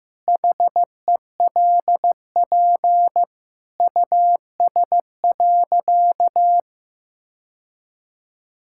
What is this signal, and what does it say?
Morse code, heard as a keyed tone, about 15 wpm, 700 Hz. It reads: HELP US.
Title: Sentences from Top 1000 Words - Review Part 5 - 15wpm